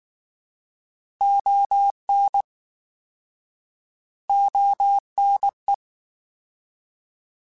Morse code, heard as a keyed tone, about 19 wpm, 800 Hz. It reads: ON ONE